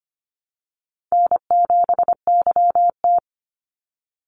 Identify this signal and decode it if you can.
Morse code, heard as a keyed tone, about 25 wpm, 700 Hz. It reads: N7YT
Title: Easy Call Signs Repeated 3x - 25wpm